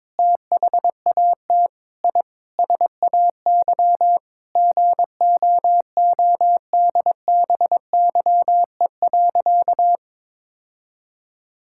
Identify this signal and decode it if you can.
Morse code, heard as a keyed tone, about 22 wpm, 700 Hz. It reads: THAT I SAY GOODBYE.
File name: Sentences with 7-Word Phrases - 22wpm